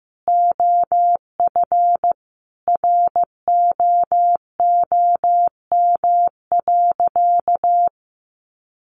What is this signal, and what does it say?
Morse code, heard as a keyed tone, about 15 wpm, 700 Hz. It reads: OF ROOM.
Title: Sentences from Top 1300 Words 15wpm